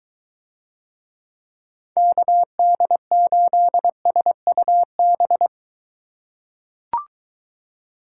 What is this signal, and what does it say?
Morse code, heard as a keyed tone, about 23 wpm, 700 Hz. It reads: KD8SUB